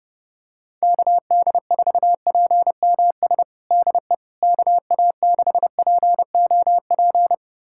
Morse code, heard as a keyed tone, about 30 wpm, 700 Hz. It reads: KD4PMS DE KA6POP
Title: USA Callsign DE Callsign 30wpm